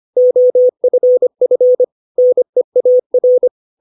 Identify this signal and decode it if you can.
Morse code, heard as a keyed tone, about 25 wpm, 500 Hz. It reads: OFF NEAR
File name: 632 words / 632 Words 25WPM